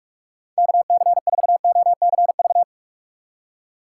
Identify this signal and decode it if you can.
Morse code, heard as a keyed tone, about 46 wpm, 700 Hz. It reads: KC4OCV
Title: USA Call Signs - 6 Characters - 2x Character Spacing - 45wpm